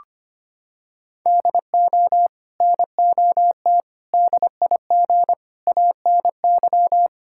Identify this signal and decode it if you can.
Morse code, heard as a keyed tone, about 25 wpm, 700 Hz. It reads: DO NOT DIG ANY
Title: Sentences from Top 1600 Words 25wpm